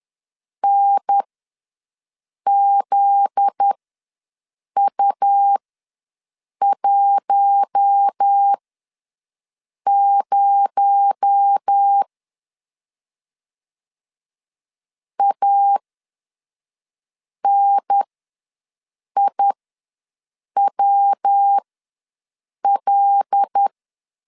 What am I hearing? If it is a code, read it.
Morse code, heard as a keyed tone, about 10 wpm, spaced as for 5 wpm, 800 Hz. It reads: NZU10 ANIWL
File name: LEZIONI / 7 GF9Z / 7 GF9Z ws 10wpm 3919